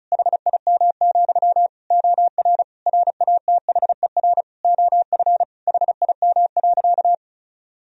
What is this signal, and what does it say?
Morse code, heard as a keyed tone, about 36 wpm, 700 Hz. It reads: HIM, OR RATHER OF HIM.
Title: Sentences from Top 1000 Words - Review Part 4 - 35wpm